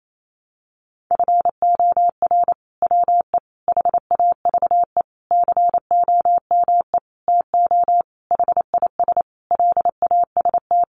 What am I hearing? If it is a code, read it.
Morse code, heard as a keyed tone, about 28 wpm, 700 Hz. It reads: FOR WE HAVE COME TO HIS LAST
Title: Sentences from Top 1000 Words 28wpm